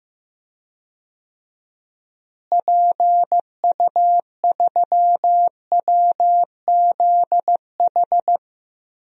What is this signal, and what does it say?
Morse code, heard as a keyed tone, about 15 wpm, 700 Hz. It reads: PU3WZH
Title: DXCC-100 Call Signs - 6 Characters - Repeated 2x - 15wpm